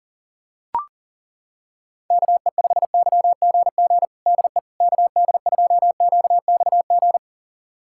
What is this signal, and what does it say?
Morse code, heard as a keyed tone, about 38 wpm, 700 Hz. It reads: KE5YGG DE KD2QXG
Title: USA Callsign DE Callsign - 6 Characters - 40wpm